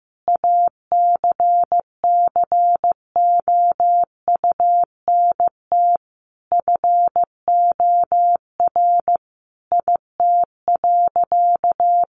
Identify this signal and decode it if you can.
Morse code, heard as a keyed tone, about 15 wpm, 700 Hz. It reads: ACCOUNT FOR IT.
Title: Sentences from Top 900 Words 15wpm